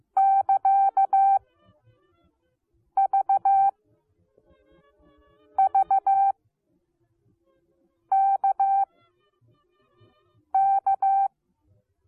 Morse code, heard as a keyed tone, about 15 wpm, 780 Hz. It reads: <KA> V V K K